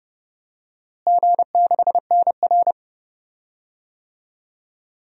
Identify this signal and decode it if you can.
Morse code, heard as a keyed tone, about 32 wpm, 700 Hz. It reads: G6NR
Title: DXCC-100 Call Signs - 4 Characters - Repeated 3x - 30wpm